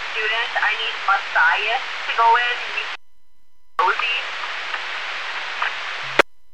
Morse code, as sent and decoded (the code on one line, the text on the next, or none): none